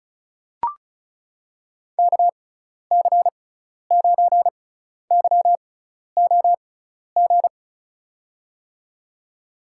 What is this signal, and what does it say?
Morse code, heard as a keyed tone, about 34 wpm, spaced as for 12 wpm, 700 Hz. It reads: KC9YOG
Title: USA Call Signs - 6 Characters - 6x Character Spacing - 35wpm